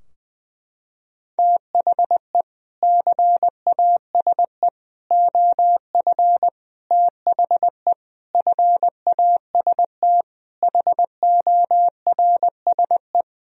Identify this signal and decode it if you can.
Morse code, heard as a keyed tone, about 20 wpm, 700 Hz. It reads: THE CASE OF THE FAST HORSE
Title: Sentences from Top 1000 Words - Review Part 2 - 20wpm